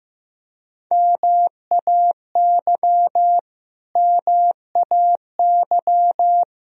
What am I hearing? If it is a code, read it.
Morse code, heard as a keyed tone, about 15 wpm, 700 Hz. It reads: MAY MAY
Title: Top 100 Words Repeated 15wpm